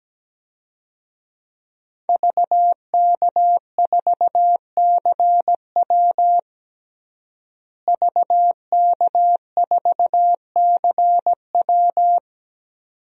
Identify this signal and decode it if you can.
Morse code, heard as a keyed tone, about 17 wpm, 700 Hz. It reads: VK4CW VK4CW
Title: DXCC-100 Call Signs - 5 Characters - Repeated 2x - 17wpm